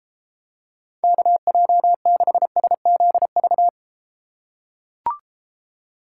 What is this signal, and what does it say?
Morse code, heard as a keyed tone, about 32 wpm, 700 Hz. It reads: KJ6SZV